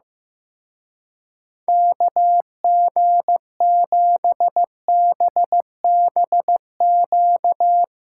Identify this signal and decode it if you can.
Morse code, heard as a keyed tone, about 16 wpm, 700 Hz. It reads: KG7BBQ
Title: Speed Racing - USA 2x3 Call Signs 15wpm